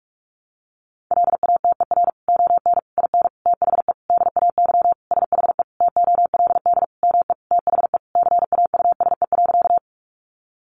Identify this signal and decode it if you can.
Morse code, heard as a keyed tone, about 44 wpm, 700 Hz. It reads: LATER ON IN THE DAY SHE TOLD ME THE CAUSE.
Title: Sentences from Top 700 Words 45wpm